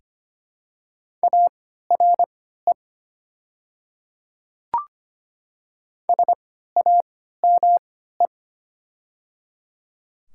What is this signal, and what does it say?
Morse code, heard as a keyed tone, about 25 wpm, spaced as for 14 wpm, 700 Hz. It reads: ARE SAME
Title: 3x Character Spacing - Top 300 Words - 25wpm